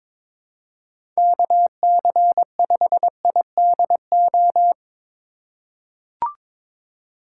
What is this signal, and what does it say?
Morse code, heard as a keyed tone, about 23 wpm, 700 Hz. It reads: KC5IDO
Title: USA Call Signs - 6 Characters - 3x Character Spacing - 22wpm